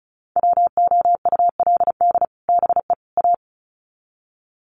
Morse code, heard as a keyed tone, about 34 wpm, 700 Hz. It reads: WOULD BE A
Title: Sets of 3 Words 35wpm